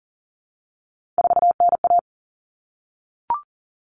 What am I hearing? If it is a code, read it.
Morse code, heard as a keyed tone, about 40 wpm, 700 Hz. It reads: 4NA
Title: Common Cut Numbers - 3-Digit 40wpm